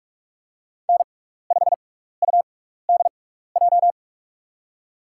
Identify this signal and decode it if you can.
Morse code, heard as a keyed tone, about 46 wpm, spaced as for 15 wpm, 700 Hz. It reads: N5UDJ
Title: USA Call Signs - 6x Character Spacing - 45wpm